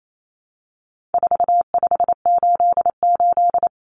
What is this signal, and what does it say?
Morse code, heard as a keyed tone, about 28 wpm, 700 Hz. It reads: 4588
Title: Common Cut Numbers - 4-Digit 28wpm